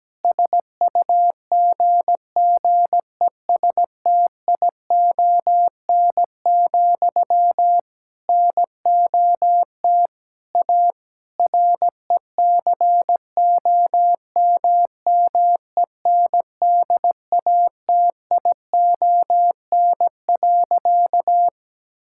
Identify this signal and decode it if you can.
Morse code, heard as a keyed tone, about 17 wpm, 700 Hz. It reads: SUGGESTION, NOT A RECOMMENDATION.
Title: Sentences from Top 1600 Words 17wpm